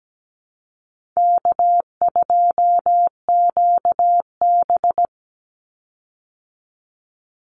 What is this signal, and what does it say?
Morse code, heard as a keyed tone, about 17 wpm, 700 Hz. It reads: K2QB
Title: Easy Call Signs Repeated 2x - 17wpm